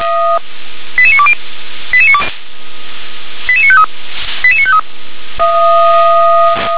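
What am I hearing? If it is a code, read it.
Five-tone selective call (ZVEI), heard as quick runs of five tones, tones of about 70 ms. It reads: 88920 88953 88953
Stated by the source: ZVEI five-tone selective call received by SDR.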